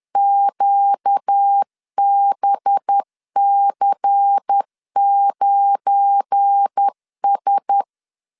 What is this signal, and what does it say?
Morse code, heard as a keyed tone, about 10 wpm, 800 Hz. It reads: QBC9S
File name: LEZIONI / 7 GF9Z / 7 GF9 10wpm 2747